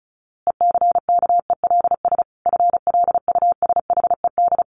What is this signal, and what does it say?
Morse code, heard as a keyed tone, about 34 wpm, 700 Hz. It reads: ECKELS FLUSHED
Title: A Sound Of Thunder 35wpm